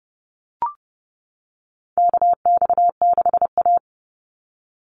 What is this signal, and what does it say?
Morse code, heard as a keyed tone, about 30 wpm, 700 Hz. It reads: KX6A
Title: NA SOTA Chaser Callsigns 30wpm